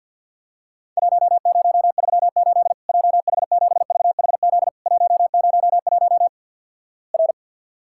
Morse code, heard as a keyed tone, about 50 wpm, 700 Hz. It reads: 1038 JH7VHZ 101 R
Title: Worked All Europe DX Contest - QTC Exchange - 50wpm